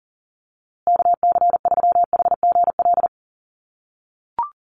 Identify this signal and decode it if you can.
Morse code, heard as a keyed tone, about 40 wpm, 700 Hz. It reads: KC3HGL
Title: USA Call Signs Repeated 2x 40wpm